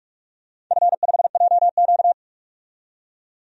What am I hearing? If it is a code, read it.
Morse code, heard as a keyed tone, about 46 wpm, 700 Hz. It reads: F5JQ
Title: DXCC-100 Call Signs - 4 Characters - 45wpm